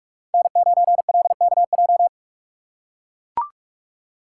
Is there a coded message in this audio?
Yes: Morse code from 0.3 to 2.1 s, heard as a keyed tone, about 43 wpm, 700 Hz. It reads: N9LKJ